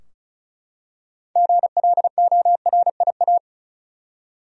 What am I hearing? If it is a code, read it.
Morse code, heard as a keyed tone, about 36 wpm, 700 Hz. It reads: GLORIA
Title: USA Common Names - 35wpm